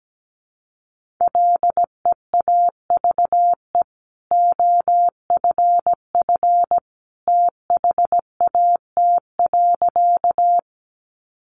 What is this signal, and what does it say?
Morse code, heard as a keyed tone, about 17 wpm, 700 Hz. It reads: LEAVE OFF THAT.
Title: Sentences from Top 400 Words 17wpm